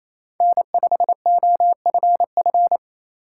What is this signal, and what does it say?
Morse code, heard as a keyed tone, about 28 wpm, 700 Hz. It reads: N5OFF
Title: USA Call Signs - 5 Characters - 4x Character Spacing - 28wpm